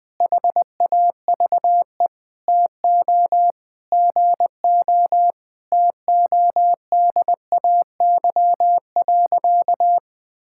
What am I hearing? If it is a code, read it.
Morse code, heard as a keyed tone, about 20 wpm, 700 Hz. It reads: HAVE TO GO TODAY.
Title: Sentences with 5-Word Phrases - Spoken Beginning - 20wpm